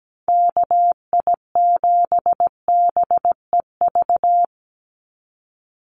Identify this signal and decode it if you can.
Morse code, heard as a keyed tone, about 17 wpm, 700 Hz. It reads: KI7BEV
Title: USA Call Signs 17wpm